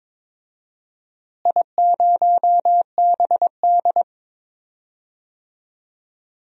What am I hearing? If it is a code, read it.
Morse code, heard as a keyed tone, about 21 wpm, 700 Hz. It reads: I0BD
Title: DXCC-100 Call Signs - 4 Characters - Repeated 2x - 22wpm